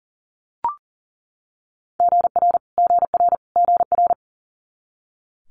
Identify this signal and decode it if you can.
Morse code, heard as a keyed tone, about 42 wpm, 700 Hz. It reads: GR GR GR
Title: Multi-letter Phonemes Repeated 3x 40wpm